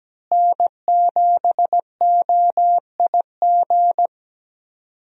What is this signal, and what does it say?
Morse code, heard as a keyed tone, about 17 wpm, 700 Hz. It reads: N7OIG
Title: USA Call Signs - 5 Characters - Repeated 3x - 17wpm